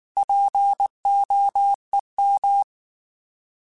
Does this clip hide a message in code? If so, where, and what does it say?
Yes: Morse code from 0.2 to 2.6 s, heard as a keyed tone, about 19 wpm, 800 Hz. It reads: POEM